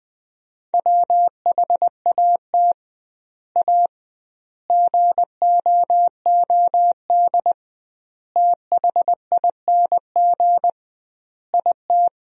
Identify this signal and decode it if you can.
Morse code, heard as a keyed tone, about 20 wpm, 700 Hz. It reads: WHAT A GOOD THING IT
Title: 2x Word-Spacing - Sentences from Top 200 Words - 20wpm